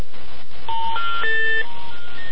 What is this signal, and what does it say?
Special information tone.